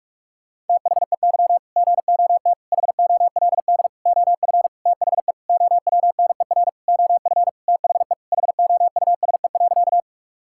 Morse code, heard as a keyed tone, about 41 wpm, 700 Hz. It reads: THEY GOT HOLD OF THE OWNER OF THE HOUSE.
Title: Sentences from Top 1600 Words 45wpm